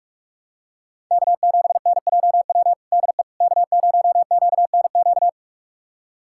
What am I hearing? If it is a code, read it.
Morse code, heard as a keyed tone, about 46 wpm, 700 Hz. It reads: K7NJW DE K0QNQ
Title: USA Callsign DE Callsign - 5 Characters - 45wpm